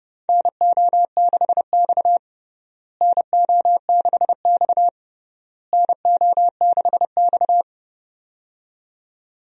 Morse code, heard as a keyed tone, about 30 wpm, 700 Hz. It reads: NO6X NO6X NO6X
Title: USA Call Signs - 4 Characters - Repeated 3x - 30wpm